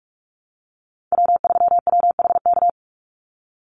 Morse code, heard as a keyed tone, about 44 wpm, 700 Hz. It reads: W3WHK
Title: Call Signs - 45wpm